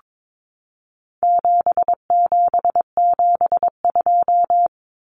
Morse code, heard as a keyed tone, about 22 wpm, 700 Hz. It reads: 7772
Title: Numbers - 4-Digit 22wpm